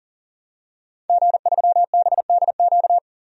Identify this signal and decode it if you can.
Morse code, heard as a keyed tone, about 40 wpm, 700 Hz. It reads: G3BDQ
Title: DXCC-100 Call Signs - 5 Characters - 40wpm